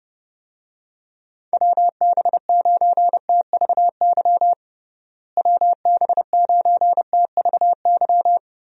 Morse code, heard as a keyed tone, about 32 wpm, 700 Hz. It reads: WB9TVY WB9TVY
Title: USA Call Signs - 6 Characters - Repeated 2x - 30wpm